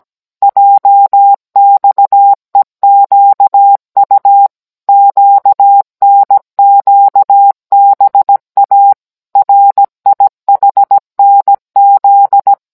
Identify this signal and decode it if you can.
Morse code, heard as a keyed tone, about 17 wpm, 800 Hz. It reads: JXEQU QNQBA RIHNZ